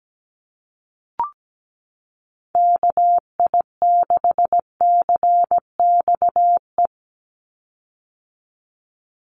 Morse code, heard as a keyed tone, about 17 wpm, 700 Hz. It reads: KI6CXE
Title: USA Call Signs 17wpm